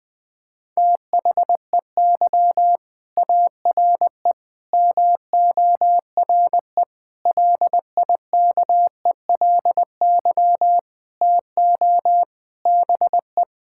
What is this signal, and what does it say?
Morse code, heard as a keyed tone, about 20 wpm, 700 Hz. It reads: THEY ARE MORE LIKELY TO BE
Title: Sentences with 5-Word Phrases - 20wpm